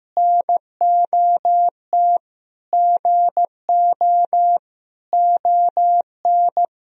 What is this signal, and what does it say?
Morse code, heard as a keyed tone, about 15 wpm, 700 Hz. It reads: NOT GO ON
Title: Sentences from Top 1000 Words - Review Part 4 - 15wpm